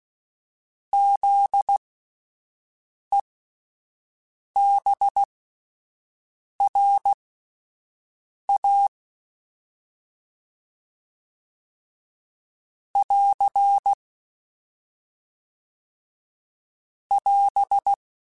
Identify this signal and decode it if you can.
Morse code, heard as a keyed tone, about 16 wpm, spaced as for 5 wpm, 800 Hz. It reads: ZEBRA <AR> <AS>
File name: Spaced Alphas / Spaced Alphas mp3s / Zs sp @10